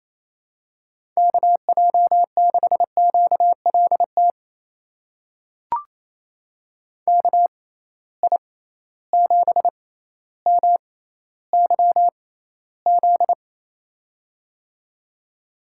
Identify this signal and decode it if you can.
Morse code, heard as a keyed tone, about 28 wpm, spaced as for 14 wpm, 700 Hz. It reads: KJ6QLT KI7MYZ
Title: USA Call Signs - 6 Characters - 6x Character Spacing - 28wpm